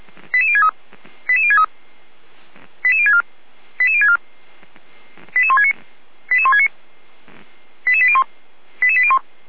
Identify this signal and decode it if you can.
Five-tone selective call (ZVEI), heard as quick runs of five tones, tones of about 70 ms. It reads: 88063 88063 88064 88064 88169 88169 88801 88801